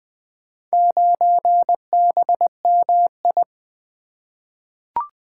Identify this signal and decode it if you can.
Morse code, heard as a keyed tone, about 20 wpm, 700 Hz. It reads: 9BMI
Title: Four Letter-Number Combinations - 20wpm